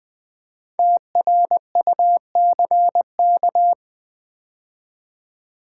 Morse code, heard as a keyed tone, about 20 wpm, 700 Hz. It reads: TRUCK
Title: 5 Letter Words - First 3 Letters Given - 20wpm